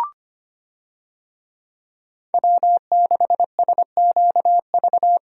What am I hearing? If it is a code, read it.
Morse code, heard as a keyed tone, about 25 wpm, 700 Hz. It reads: W6SQV
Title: USA Call Signs - 5 Characters - Repeated 3x - 25wpm